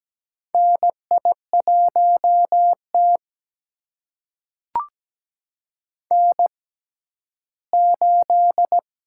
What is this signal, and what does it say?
Morse code, heard as a keyed tone, about 17 wpm, spaced as for 11 wpm, 700 Hz. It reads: NI1T N8
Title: USA Call Signs - 6x Character Spacing - 17wpm